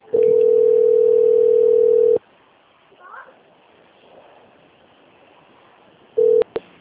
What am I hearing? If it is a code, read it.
Ringback tone (North America), 440 + 480 Hz, with speech.
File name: out- - - - - .53355